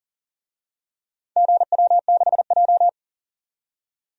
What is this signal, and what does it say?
Morse code, heard as a keyed tone, about 40 wpm, 700 Hz. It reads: GW6J
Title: DXCC-100 Call Signs - 4 Characters - 40wpm